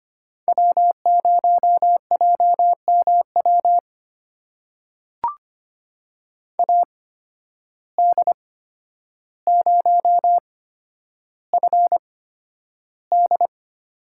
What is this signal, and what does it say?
Morse code, heard as a keyed tone, about 25 wpm, spaced as for 11 wpm, 700 Hz. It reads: W0JMW AD0FD
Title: USA Call Signs - 5 Characters - 8x Character Spacing - 25wpm